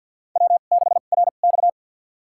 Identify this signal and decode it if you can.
Morse code, heard as a keyed tone, about 50 wpm, spaced as for 36 wpm, 700 Hz. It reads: W6RX
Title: USA Call Signs - 4 Characters - 2x Character Spacing - 50wpm